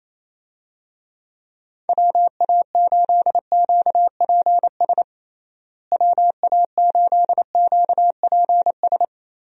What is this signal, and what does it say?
Morse code, heard as a keyed tone, about 28 wpm, 700 Hz. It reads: WA8QPS WA8QPS
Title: USA Call Signs - 6 Characters - Repeated 3x - 28wpm